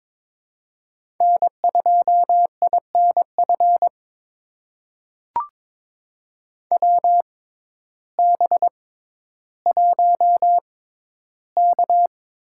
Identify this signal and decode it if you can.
Morse code, heard as a keyed tone, about 22 wpm, spaced as for 12 wpm, 700 Hz. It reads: N2INF WB1K
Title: USA Call Signs - 6x Character Spacing - 22wpm